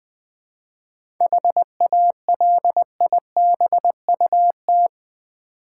Morse code, heard as a keyed tone, about 20 wpm, 690 Hz. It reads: HALIBUT
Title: Difficult Words with Hints - 20wpm